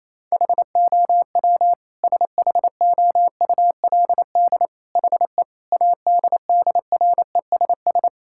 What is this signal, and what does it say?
Morse code, heard as a keyed tone, about 28 wpm, 700 Hz. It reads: HOW SHOULD HE ADDRESS
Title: Sentences from Top 1000 Words - Review Part 6 - 28wpm